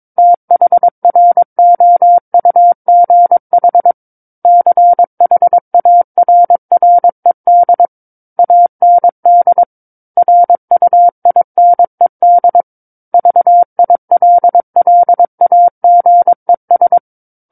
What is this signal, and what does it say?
Morse code, heard as a keyed tone, about 22 wpm, 700 Hz. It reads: THROUGH CHARRED AND RUINED VILLAGES
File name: Book One / 13-Chapter 13 - 22 WPM